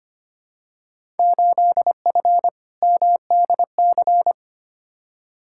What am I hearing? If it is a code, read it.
Morse code, heard as a keyed tone, about 25 wpm, 700 Hz. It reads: 8F MDC
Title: Field Day Exchanges Encore - 25wpm